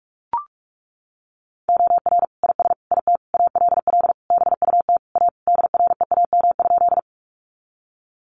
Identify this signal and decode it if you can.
Morse code, heard as a keyed tone, about 44 wpm, 700 Hz. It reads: OR IS IT ALL BUT A DREAM?
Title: Sentences from Top 1500 Words 45wpm